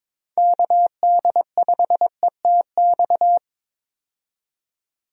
Morse code, heard as a keyed tone, about 22 wpm, 700 Hz. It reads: KD5ETX
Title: USA Call Signs - 6 Characters - 22wpm